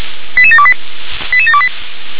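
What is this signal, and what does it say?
Five-tone selective call (ZVEI), heard as quick runs of five tones, tones of about 70 ms. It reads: 88728 88728
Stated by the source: ZVEI five-tone selective call received by SDR.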